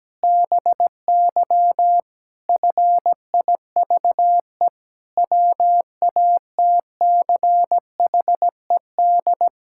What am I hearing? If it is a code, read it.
Morse code, heard as a keyed tone, about 17 wpm, 700 Hz. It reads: BY FIVE WATCHED
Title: Pangram Challenge v2 - 17wpm